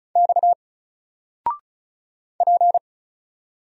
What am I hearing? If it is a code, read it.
Morse code, heard as a keyed tone, about 36 wpm, 700 Hz. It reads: X P
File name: Single-Letter-Number-35wpm